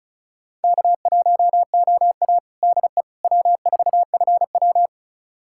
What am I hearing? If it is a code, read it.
Morse code, heard as a keyed tone, about 34 wpm, 700 Hz. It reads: K1OA DE W4FW
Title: USA Callsign DE Callsign - 4 Characters - 35wpm